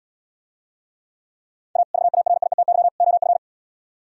Morse code, heard as a keyed tone, about 77 wpm, 700 Hz. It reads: A VARIETY OF